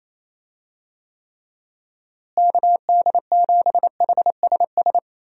Morse code, heard as a keyed tone, about 28 wpm, 700 Hz. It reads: KD7HSS
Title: USA Call Signs - 6 Characters - Repeated 2x - 28wpm